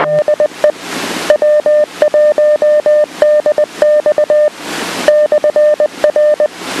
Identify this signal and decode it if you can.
Morse code, heard as a keyed tone, about 20 wpm, 600 Hz. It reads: DE W1DX /R